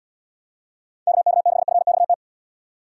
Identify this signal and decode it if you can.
Morse code, heard as a keyed tone, about 69 wpm, 700 Hz. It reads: KK6FXT